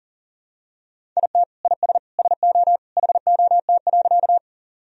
Morse code, heard as a keyed tone, about 40 wpm, 700 Hz. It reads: IT IS SO HOT.
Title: Sentences from Top 1000 Words - Review Part 3 - 40wpm